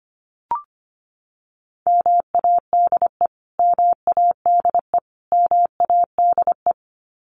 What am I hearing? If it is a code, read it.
Morse code, heard as a keyed tone, about 25 wpm, 700 Hz. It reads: MADE MADE MADE
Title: Top 300 Words Repeated 25wpm